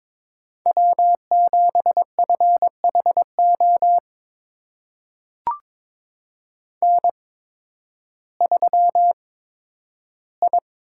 Morse code, heard as a keyed tone, about 22 wpm, spaced as for 12 wpm, 700 Hz. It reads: W7FHO N3I
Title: USA Call Signs - 5 Characters - 8x Character Spacing - 22wpm